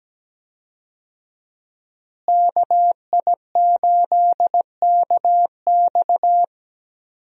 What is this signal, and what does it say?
Morse code, heard as a keyed tone, about 17 wpm, 700 Hz. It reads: KI8KX
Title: USA Call Signs - 5 Characters - Repeated 3x - 17wpm